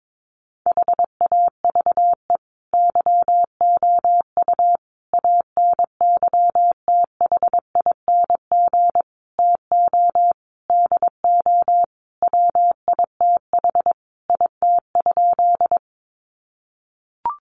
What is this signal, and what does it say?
Morse code, heard as a keyed tone, about 22 wpm, 700 Hz. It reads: HAVE YOU ANYTHING TO DO WITH IT?
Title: Sentences from Top 1000 Words 22wpm